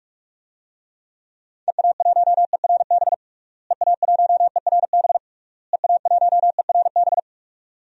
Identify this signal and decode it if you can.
Morse code, heard as a keyed tone, about 46 wpm, 700 Hz. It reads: EA1ERB EA1ERB EA1ERB
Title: DXCC-100 Call Signs - 6 Characters - Repeated 3x - 45wpm